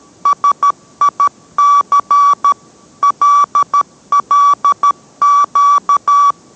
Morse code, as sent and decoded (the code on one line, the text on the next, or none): ... .. -.-. / .-.. .-.. --.-
SIC LLQ